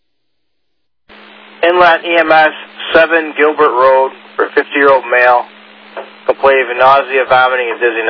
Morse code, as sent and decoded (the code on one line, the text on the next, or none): none